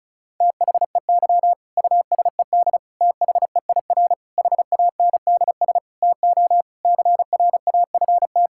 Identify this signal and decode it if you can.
Morse code, heard as a keyed tone, about 34 wpm, 700 Hz. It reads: THEY USED THEIR HANDS TO CRAFT
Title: Sentences from Top 1000 Words - Review Part 6 - 35wpm